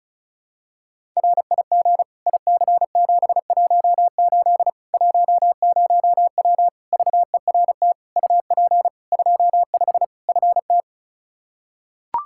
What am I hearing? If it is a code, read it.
Morse code, heard as a keyed tone, about 36 wpm, 700 Hz. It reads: RIG IC718 10W VERT UP 25 FT
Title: Practice QSO - Exchange 2 - Info - 35wpm